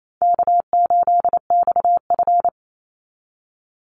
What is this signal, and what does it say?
Morse code, heard as a keyed tone, about 28 wpm, 700 Hz. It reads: K8XF
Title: Call Signs Repeated 3x - 28wpm